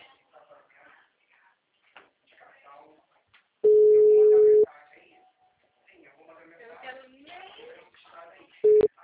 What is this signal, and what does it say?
Ringback tone (Europe, 425 Hz), with speech.
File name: out- -2024- - - .84956